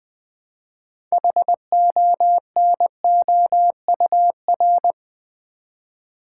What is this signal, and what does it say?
Morse code, heard as a keyed tone, about 20 wpm, 700 Hz. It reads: HONOUR
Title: Build A Word - 2 Letters At A Time - 6-Letter Words - 20wpm